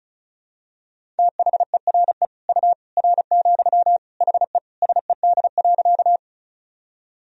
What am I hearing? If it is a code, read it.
Morse code, heard as a keyed tone, about 36 wpm, 700 Hz. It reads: THERE U R, HE SED.
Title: Sentences from Top 100 Words and Abbreviations 35wpm